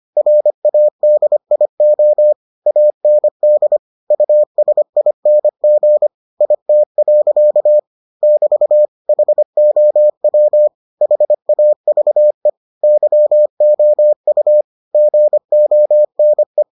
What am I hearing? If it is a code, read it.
Morse code, heard as a keyed tone, about 25 wpm, 600 Hz. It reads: RADIO AND USING IT. <BT> HOW HAVE YOU GONE